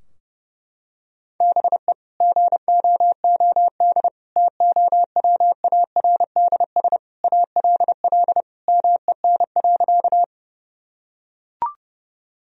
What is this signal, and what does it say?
Morse code, heard as a keyed tone, about 30 wpm, 700 Hz. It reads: BE GOOD TOWARDS ALL MEN.
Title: Sentences from Top 1000 Words - Review Part 2 - 30wpm